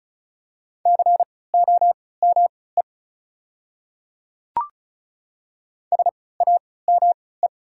Morse code, heard as a keyed tone, about 35 wpm, spaced as for 20 wpm, 690 Hz. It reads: COME SAME
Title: 3x Character Spacing - Top 200 Words - 35wpm